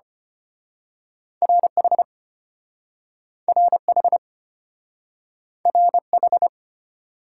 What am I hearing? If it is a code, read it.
Morse code, heard as a keyed tone, about 34 wpm, 700 Hz. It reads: RH RH RH